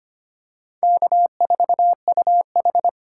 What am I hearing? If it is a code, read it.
Morse code, heard as a keyed tone, about 25 wpm, 700 Hz. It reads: K4UH
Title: USA Call Signs - 4 Characters - 4x Character Spacing - 25wpm